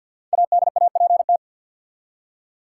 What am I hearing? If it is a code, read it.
Morse code, heard as a keyed tone, about 52 wpm, 700 Hz. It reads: ADAPT
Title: Five Letter Words - 50wpm